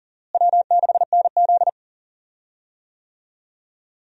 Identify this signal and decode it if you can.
Morse code, heard as a keyed tone, about 40 wpm, 700 Hz. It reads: W6NZ
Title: USA Call Signs - 4 Characters - Repeated 2x - 40wpm